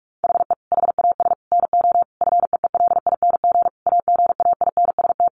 Morse code, heard as a keyed tone, about 44 wpm, 700 Hz. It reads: HE HAS NO FEELING AGAINST
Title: Sentences from Top 1200 Words 45wpm